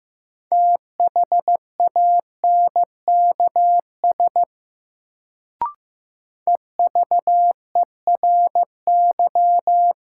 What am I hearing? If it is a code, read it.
Morse code, heard as a keyed tone, about 15 wpm, 700 Hz. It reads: THANKS EVERY
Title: Sets of 5 Words - New Word Drill - 15wpm